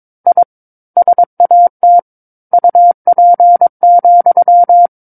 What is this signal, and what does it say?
Morse code, heard as a keyed tone, about 22 wpm, 700 Hz. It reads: I SAT UP,
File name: Book One / 13-Chapter 13 - 22 WPM